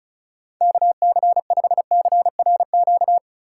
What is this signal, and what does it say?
Morse code, heard as a keyed tone, about 36 wpm, 700 Hz. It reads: KC5CRQ
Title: USA Call Signs - 6 Characters - 4x Character Spacing - 35wpm